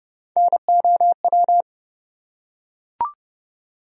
Morse code, heard as a keyed tone, about 32 wpm, 700 Hz. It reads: NOW
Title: Top 100 Words Repeated 30wpm